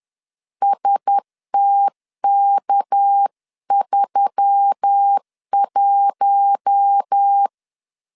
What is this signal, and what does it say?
Morse code, heard as a keyed tone, about 10 wpm, 800 Hz. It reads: STK31
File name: LEZIONI / 6 PKX0 / 6 PK 10wpm 2436